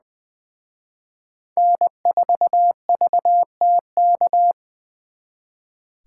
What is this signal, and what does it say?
Morse code, heard as a keyed tone, about 24 wpm, 700 Hz. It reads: N4VTK
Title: Speed Racing - USA 1x3 Call Signs 20wpm